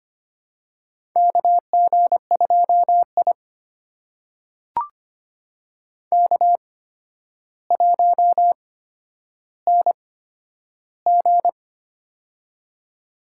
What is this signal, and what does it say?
Morse code, heard as a keyed tone, about 25 wpm, spaced as for 11 wpm, 700 Hz. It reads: KG2I K1NG
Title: USA Call Signs - 4 Characters - 8x Character Spacing - 25wpm